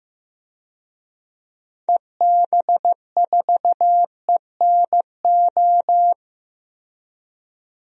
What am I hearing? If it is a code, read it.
Morse code, heard as a keyed tone, about 15 wpm, 700 Hz. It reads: EB4ENO
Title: DXCC-100 Call Signs - 6 Characters - Repeated 2x - 15wpm